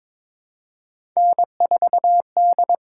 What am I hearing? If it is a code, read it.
Morse code, heard as a keyed tone, about 22 wpm, 700 Hz. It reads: N4D
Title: USA Call Signs - 5 Characters - 6x Character Spacing - 22wpm